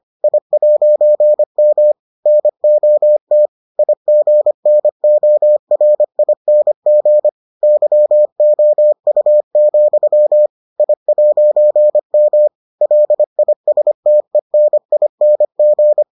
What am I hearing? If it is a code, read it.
Morse code, heard as a keyed tone, about 25 wpm, 600 Hz. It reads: I'M NOT IGNORING YOU, I'M LISTENING